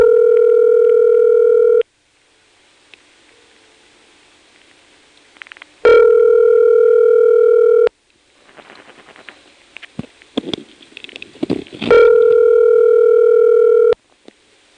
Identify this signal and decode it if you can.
Ringback tone (North America), 440 + 480 Hz.
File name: phonetrips / SmyrnaGA5HQ